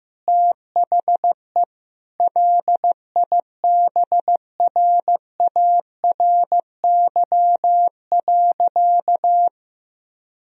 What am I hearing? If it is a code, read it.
Morse code, heard as a keyed tone, about 15 wpm, 700 Hz. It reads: THE LIBRARY.
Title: Sentences from Top 1000 Words - Review Part 4 - 15wpm